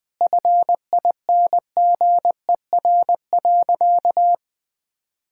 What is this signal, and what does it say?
Morse code, heard as a keyed tone, about 21 wpm, 700 Hz. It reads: FINGER.
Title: Sentences from Top 1600 Words 20wpm